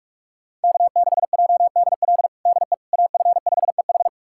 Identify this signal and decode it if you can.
Morse code, heard as a keyed tone, about 46 wpm, 700 Hz. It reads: K6JDL DE AF5EH
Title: USA Callsign DE Callsign - 5 Characters - 45wpm